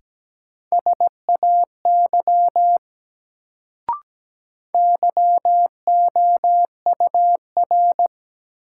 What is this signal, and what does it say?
Morse code, heard as a keyed tone, about 17 wpm, 700 Hz. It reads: SAY YOUR
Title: Sets of 2 Words - New Word Drill - 17wpm